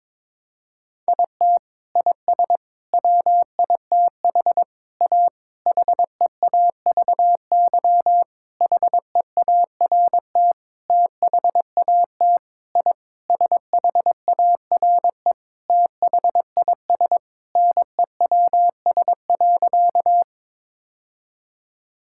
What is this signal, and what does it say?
Morse code, heard as a keyed tone, about 22 wpm, 700 Hz. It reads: IT IS WITH A HEAVY HEART THAT I SHARE THIS NEWS.
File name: practice / 22wpm / Sentences with 8-Word Phrases - 22wpm